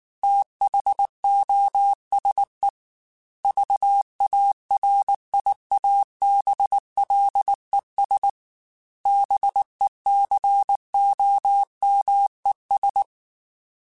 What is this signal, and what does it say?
Morse code, heard as a keyed tone, about 19 wpm, 800 Hz. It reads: THOSE VARIABLES BECOMES
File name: Technical Subjects / Technical Subjects-mp3s / AND Gate @18